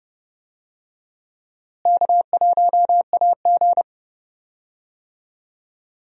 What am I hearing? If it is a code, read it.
Morse code, heard as a keyed tone, about 30 wpm, 700 Hz. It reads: K1AG
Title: USA Call Signs - 4 Characters - Repeated 3x - 30wpm